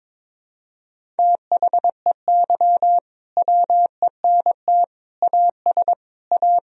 Morse code, heard as a keyed tone, about 22 wpm, 700 Hz. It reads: THEY WENT AS A
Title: Sentences from Top 1000 Words - Review Part 1 - 22wpm